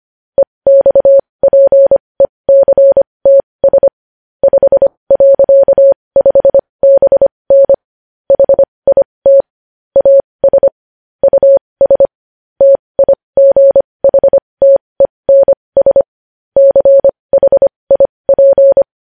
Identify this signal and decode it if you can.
Morse code, heard as a keyed tone, about 25 wpm, spaced as for 20 wpm, 550 Hz. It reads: EXPECTS 5.5BN HIT AS US TIGHTENS CHIP
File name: podcast / News headlines-1x-25x20